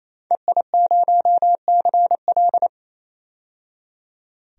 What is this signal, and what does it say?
Morse code, heard as a keyed tone, about 28 wpm, 700 Hz. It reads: EI0CL
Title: DXCC-100 Call Signs - 5 Characters - 28wpm